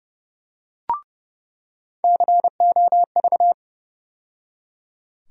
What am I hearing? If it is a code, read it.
Morse code, heard as a keyed tone, about 31 wpm, 700 Hz. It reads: COV
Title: USA Call Sign Postfixes - 3 Letters 30wpm